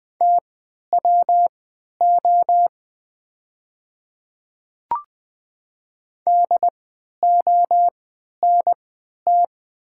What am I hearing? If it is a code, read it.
Morse code, heard as a keyed tone, about 20 wpm, spaced as for 11 wpm, 690 Hz. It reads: TWO DONT
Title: 3x Character Spacing - Top 300 Words - 20wpm